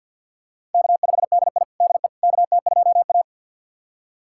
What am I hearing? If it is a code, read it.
Morse code, heard as a keyed tone, about 51 wpm, 700 Hz. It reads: K5DI DE KT2A
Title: USA Callsign DE Callsign - 4 Characters - 50wpm